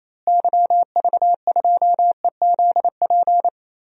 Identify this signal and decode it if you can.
Morse code, heard as a keyed tone, about 28 wpm, 700 Hz. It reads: YV2EZP
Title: DXCC-100 Call Signs - 6 Characters - Repeated 3x - 28wpm